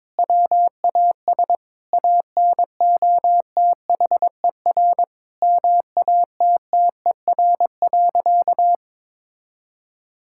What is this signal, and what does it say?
Morse code, heard as a keyed tone, about 22 wpm, 700 Hz. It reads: WAS ANOTHER MATTER.
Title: Sentences from Top 1000 Words - Review Part 5 - 22wpm